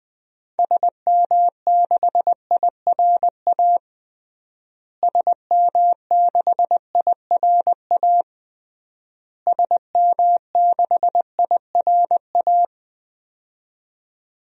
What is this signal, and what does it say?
Morse code, heard as a keyed tone, about 20 wpm, 700 Hz. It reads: SM6IRA SM6IRA SM6IRA
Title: DXCC-100 Call Signs - 6 Characters - Repeated 3x - 20wpm